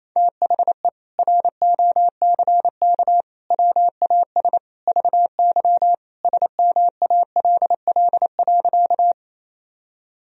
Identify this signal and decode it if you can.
Morse code, heard as a keyed tone, about 28 wpm, 700 Hz. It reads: THE ROCK WAS VY SMALL.
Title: Sentences from Top 400 Words and Abbreviations 28wpm